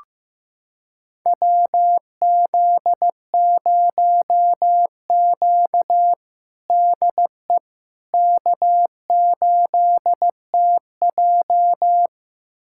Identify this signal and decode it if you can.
Morse code, heard as a keyed tone, about 15 wpm, 700 Hz. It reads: WZ0Q DE K8TJ
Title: USA Callsign DE Callsign - 4 Characters - 15wpm